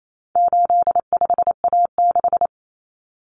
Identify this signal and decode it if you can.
Morse code, heard as a keyed tone, about 28 wpm, 700 Hz. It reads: 85A6
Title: Common Cut Numbers - 4-Digit 28wpm